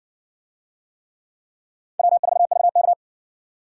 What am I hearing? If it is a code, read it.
Morse code, heard as a keyed tone, about 55 wpm, 700 Hz. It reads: W4VK